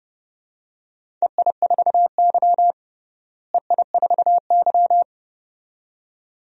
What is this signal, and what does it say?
Morse code, heard as a keyed tone, about 30 wpm, 700 Hz. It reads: EI4Y EI4Y
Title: DXCC-100 Call Signs - 4 Characters - Repeated 2x - 30wpm